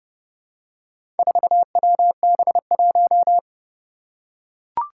Morse code, heard as a keyed tone, about 32 wpm, 700 Hz. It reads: 4WB1
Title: Four Challenging Letter-Number Combinations 30wpm